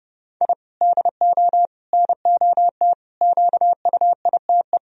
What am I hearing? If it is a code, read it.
Morse code, heard as a keyed tone, about 31 wpm, 700 Hz. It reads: I DO NOT QUITE
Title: Sentences from Top 1000 Words - Review Part 1 - 30wpm